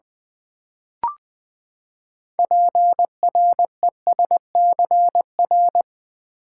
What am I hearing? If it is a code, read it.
Morse code, heard as a keyed tone, about 20 wpm, 700 Hz. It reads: PRESCR
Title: Build A Word - 6 Letters At A Time - 12-Letter Words - 20wpm